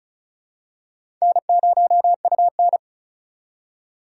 Morse code, heard as a keyed tone, about 34 wpm, 700 Hz. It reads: N0UN
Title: USA Call Signs - 4 Characters - 3x Character Spacing - 35wpm